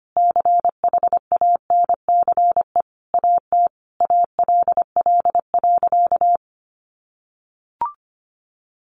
Morse code, heard as a keyed tone, about 25 wpm, 700 Hz. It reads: CHANCE AT ALL.
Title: Sentences from Top 800 Words 25wpm